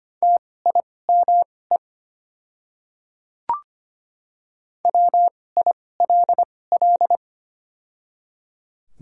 Morse code, heard as a keyed tone, about 25 wpm, spaced as for 18 wpm, 700 Hz. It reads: TIME WILL
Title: 2x Character Spacing - Top 100 Words - 25wpm